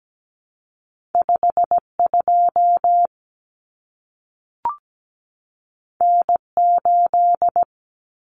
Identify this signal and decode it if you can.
Morse code, heard as a keyed tone, about 17 wpm, 700 Hz. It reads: 52 N8
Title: Common Cut Numbers - 2-Digit 17wpm